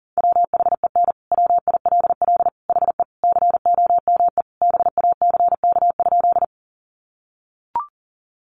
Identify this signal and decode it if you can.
Morse code, heard as a keyed tone, about 40 wpm, 700 Hz. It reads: WHEN WILL HE COME BACK?
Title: Sentences from Top 200 Words 40wpm